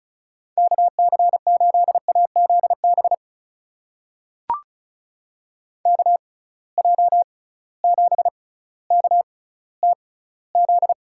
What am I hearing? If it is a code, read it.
Morse code, heard as a keyed tone, about 36 wpm, spaced as for 18 wpm, 700 Hz. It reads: KC8AZB KJ7KTZ
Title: USA Call Signs - 6x Character Spacing - 35wpm